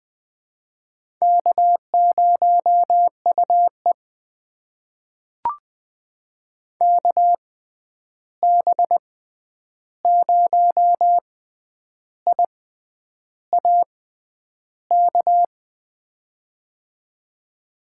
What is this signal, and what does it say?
Morse code, heard as a keyed tone, about 20 wpm, spaced as for 9 wpm, 700 Hz. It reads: K0UE KB0IAK
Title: USA Call Signs - 6x Character Spacing - 20wpm